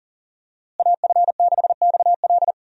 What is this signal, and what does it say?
Morse code, heard as a keyed tone, about 40 wpm, 700 Hz. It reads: AF6XL
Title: USA Call Signs - 5 Characters - Repeated 3x - 40wpm